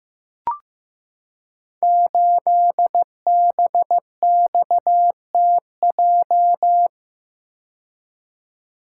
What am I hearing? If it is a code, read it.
Morse code, heard as a keyed tone, about 15 wpm, 700 Hz. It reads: 8BXTJ
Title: Code Groups - 5 Letter:Numbers - 15wpm